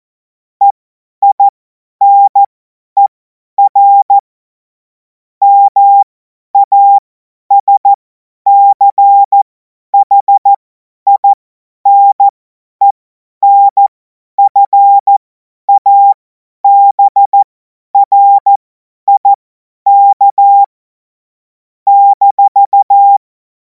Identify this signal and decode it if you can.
Morse code, heard as a keyed tone, about 14 wpm, spaced as for 10 wpm, 800 Hz. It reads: EINER MASCHINENFABRIK -
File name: Morsekurs / Witze / Witz 03. März 2026 50 F70